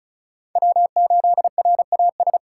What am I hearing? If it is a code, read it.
Morse code, heard as a keyed tone, about 34 wpm, 700 Hz. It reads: W8RAS
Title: USA Call Signs - 5 Characters - 2x Character Spacing - 35wpm